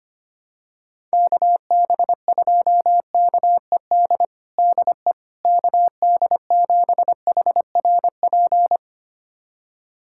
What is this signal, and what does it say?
Morse code, heard as a keyed tone, about 25 wpm, 700 Hz. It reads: KB2KED DE KD7HRP
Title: USA Callsign DE Callsign 25wpm